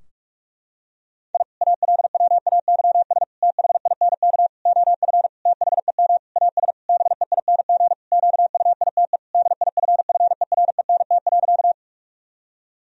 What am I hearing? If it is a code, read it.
Morse code, heard as a keyed tone, about 41 wpm, 700 Hz. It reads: I ALWAYS THINK OF THEM AS BEING QUITE DIFFERENT.
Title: Sentences from Top 1000 Words - Review Part 3 - 45wpm